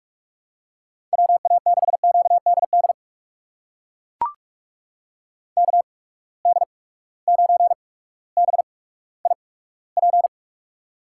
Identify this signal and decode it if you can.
Morse code, heard as a keyed tone, about 43 wpm, spaced as for 19 wpm, 700 Hz. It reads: WA6QDD KD9BIP
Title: USA Call Signs - 6 Characters - 8x Character Spacing - 45wpm